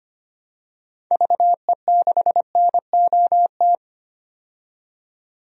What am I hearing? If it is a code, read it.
Morse code, heard as a keyed tone, about 25 wpm, 690 Hz. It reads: VE6NOT
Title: DXCC-100 Call Signs - 6 Characters - 25wpm